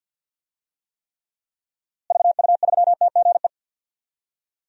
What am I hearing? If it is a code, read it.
Morse code, heard as a keyed tone, about 52 wpm, 700 Hz. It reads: VU3TGE
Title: DXCC-100 Call Signs - 6 Characters - Repeated 2x - 50wpm